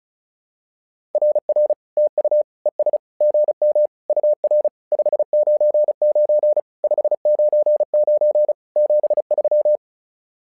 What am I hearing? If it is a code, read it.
Morse code, heard as a keyed tone, about 33 wpm, 600 Hz. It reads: RR TU ES GM UR 599 599 73
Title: SOTA QSO Practice 35wpm